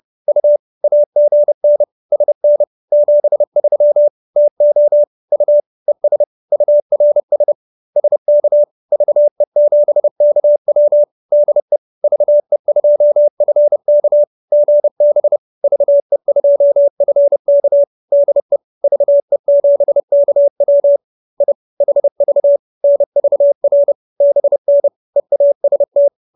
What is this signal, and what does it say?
Morse code, heard as a keyed tone, about 30 wpm, 590 Hz. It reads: U AGN SN 73 TO U ES URS SK VE7KW DE VE2FK GB VE2FK DE VE7KW I HV NVR BN EAST